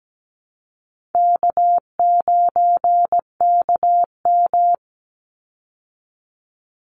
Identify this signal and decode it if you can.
Morse code, heard as a keyed tone, about 17 wpm, 700 Hz. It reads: K9KM
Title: Easy Call Signs Repeated 3x - 17wpm